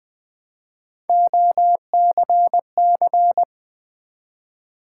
Morse code, heard as a keyed tone, about 20 wpm, 700 Hz. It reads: OCC
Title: Build A Word - 1 Letter At A Time - 5-Letter Words - 20wpm